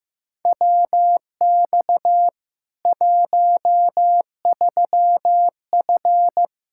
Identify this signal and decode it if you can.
Morse code, heard as a keyed tone, about 15 wpm, 700 Hz. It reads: WX 13F
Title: K1USN SST QSO Practice 15wpm